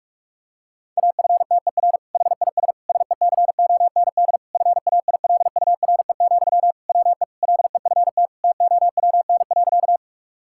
Morse code, heard as a keyed tone, about 41 wpm, 700 Hz. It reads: AFTER HIS SECOND FAILURE, WE LEFT TOWN.
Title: Sentences from Top 1000 Words - Review Part 6 - 45wpm